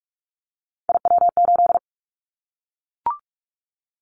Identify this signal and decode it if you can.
Morse code, heard as a keyed tone, about 44 wpm, 700 Hz. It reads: IW8
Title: Callsign Prefixes for Top 50 DXCC - 45wpm